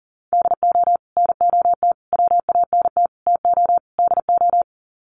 Morse code, heard as a keyed tone, about 38 wpm, 700 Hz. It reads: DO NOT WANT TO DO
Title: Sets of 5 Words 40wpm